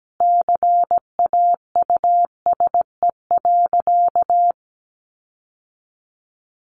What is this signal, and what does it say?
Morse code, heard as a keyed tone, about 17 wpm, 700 Hz. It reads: CAUSE.
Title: Sentences from Top 1400 Words 17wpm